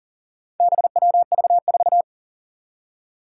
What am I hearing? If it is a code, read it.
Morse code, heard as a keyed tone, about 40 wpm, 700 Hz. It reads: BWV4
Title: Four Challenging Letter-Number Combinations 40wpm